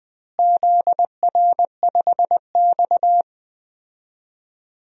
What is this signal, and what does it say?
Morse code, heard as a keyed tone, about 21 wpm, 700 Hz. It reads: ZR5X
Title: DXCC-100 Call Signs - 4 Characters - Repeated 3x - 20wpm